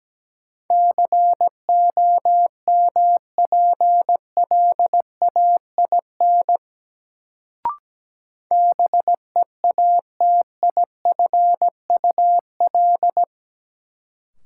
Morse code, heard as a keyed tone, about 17 wpm, 700 Hz. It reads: COMPLAIN BEATIFUL
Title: Sentences from Top 1300 Words - New Word Drill Alt - 17wpm